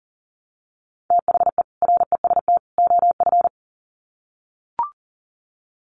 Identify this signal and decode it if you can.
Morse code, heard as a keyed tone, about 38 wpm, 700 Hz. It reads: THE REST OF
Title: Sets of 3 Words 40wpm